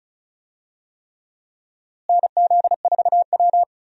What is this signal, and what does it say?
Morse code, heard as a keyed tone, about 36 wpm, 700 Hz. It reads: NZ4W
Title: USA Call Signs - 4 Characters - Repeated 3x - 35wpm